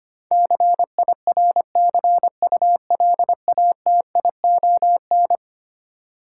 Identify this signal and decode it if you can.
Morse code, heard as a keyed tone, about 25 wpm, 700 Hz. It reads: CIRCULATION
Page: morse-code-ninja-practice-25wpm